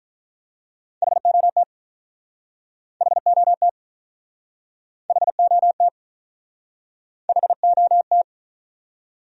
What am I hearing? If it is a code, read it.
Morse code, heard as a keyed tone, about 44 wpm, 700 Hz. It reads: HOT HOT HOT HOT